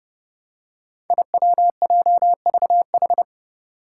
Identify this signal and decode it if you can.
Morse code, heard as a keyed tone, about 30 wpm, 700 Hz. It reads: IWJVH
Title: Five Challenging Letter-Number Combinations 30wpm